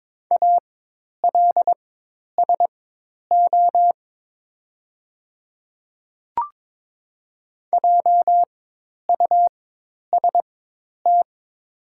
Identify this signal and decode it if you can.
Morse code, heard as a keyed tone, about 22 wpm, spaced as for 10 wpm, 700 Hz. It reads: ALSO JUST
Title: 4x Character Spacing - Top 200 Words - 22wpm